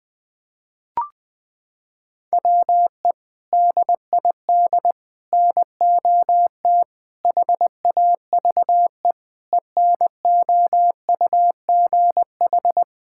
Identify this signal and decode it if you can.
Morse code, heard as a keyed tone, about 20 wpm, 700 Hz. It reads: WE DID NOT HAVE ENOUGH
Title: Sentences with 4-Word Phrases - 20wpm